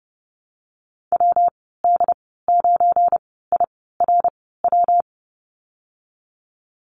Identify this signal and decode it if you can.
Morse code, heard as a keyed tone, about 32 wpm, spaced as for 17 wpm, 700 Hz. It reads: WD9IRW
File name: Lesson 37g - Letter x - Call Signs 30wpm